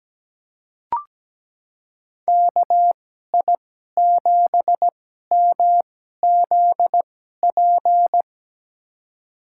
Morse code, heard as a keyed tone, about 17 wpm, spaced as for 12 wpm, 700 Hz. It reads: KI7MZP
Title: USA Call Signs - 6 Characters - 2x Character Spacing - 17wpm